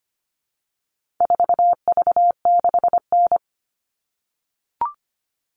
Morse code, heard as a keyed tone, about 25 wpm, 700 Hz. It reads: 4V6N
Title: All Cut Numbers - 4-Digit 25wpm